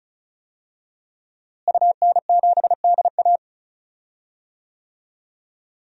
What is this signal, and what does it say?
Morse code, heard as a keyed tone, about 36 wpm, 700 Hz. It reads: UN7DA
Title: DXCC-100 Call Signs - 5 Characters - Repeated 2x - 35wpm